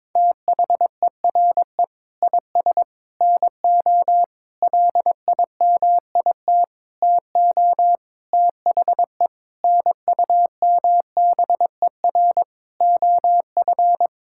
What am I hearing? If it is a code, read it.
Morse code, heard as a keyed tone, about 22 wpm, 700 Hz. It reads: THERE IS NO LIMIT TO THE NUMBER OF
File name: practice / 22wpm / Sentences with 8-Word Phrases - 22wpm